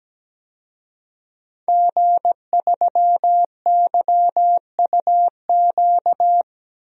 Morse code, heard as a keyed tone, about 17 wpm, 700 Hz. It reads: G3YUQ
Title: DXCC-100 Call Signs - 5 Characters - Repeated 3x - 17wpm